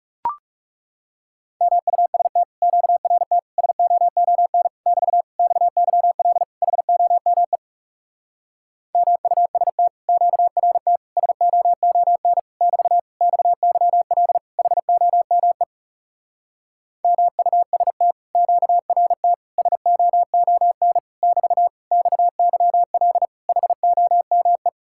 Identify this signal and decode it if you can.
Morse code, heard as a keyed tone, about 37 wpm, 700 Hz. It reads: MUST QRT SOON <BT> XYL HOME MUST QRT SOON <BT> XYL HOME MUST QRT SOON <BT> XYL HOME
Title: Speed-Racing Common QSO Phrases Encore - 30wpm